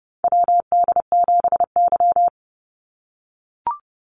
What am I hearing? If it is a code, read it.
Morse code, heard as a keyed tone, about 30 wpm, 700 Hz. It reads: WD7Y
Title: NA SOTA Chaser Callsigns 30wpm